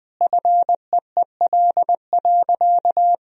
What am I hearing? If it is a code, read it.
Morse code, heard as a keyed tone, about 21 wpm, 700 Hz. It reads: FEEL.
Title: Sentences from Top 1000 Words - Review Part 3 - 20wpm